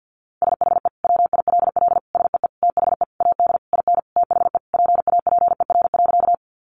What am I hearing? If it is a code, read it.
Morse code, heard as a keyed tone, about 50 wpm, 700 Hz. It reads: SHE WILL SEE THE AD IN THE PAPER.
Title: Sentences from Top 1300 Words 50wpm